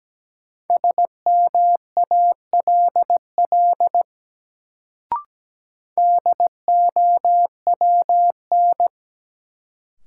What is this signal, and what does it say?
Morse code, heard as a keyed tone, about 17 wpm, 700 Hz. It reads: SMALL DOWN
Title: Sets of 2 Words - New Word Drill - 17wpm